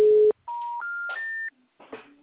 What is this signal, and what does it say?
Special information tone.
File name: out- -2037- - - .7